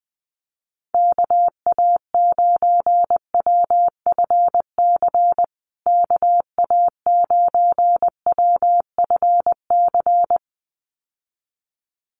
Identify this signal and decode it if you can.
Morse code, heard as a keyed tone, about 20 wpm, 700 Hz. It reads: KA9WFC KA9WFC
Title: USA Call Signs Repeated 2x 20wpm